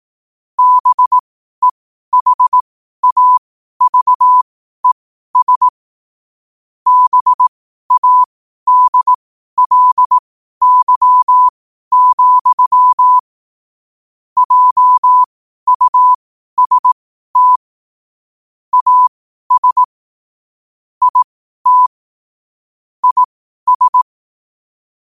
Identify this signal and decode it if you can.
Morse code, heard as a keyed tone, about 18 wpm, spaced as for 12 wpm, 1000 Hz. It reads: BEHAVES BADLY, JUST AS IT IS
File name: qotdfiles / 12wpm